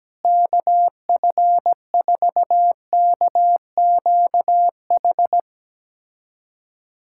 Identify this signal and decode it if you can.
Morse code, heard as a keyed tone, about 17 wpm, 700 Hz. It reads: KF4KQH
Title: USA Callsign DE Callsign - 6 Characters - 17wpm